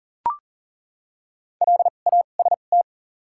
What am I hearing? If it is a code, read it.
Morse code, heard as a keyed tone, about 40 wpm, spaced as for 29 wpm, 700 Hz. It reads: LAST
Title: 2x Character Spacing - Top 500 Words - 40wpm